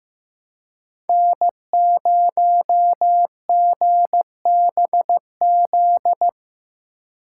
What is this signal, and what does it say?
Morse code, heard as a keyed tone, about 17 wpm, 700 Hz. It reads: N0GBZ
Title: Speed Racing - USA 1x3 Call Signs 15wpm